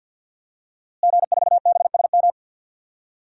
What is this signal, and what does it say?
Morse code, heard as a keyed tone, about 51 wpm, 700 Hz. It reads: G4BSM
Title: DXCC-100 Call Signs - 5 Characters - Repeated 3x - 50wpm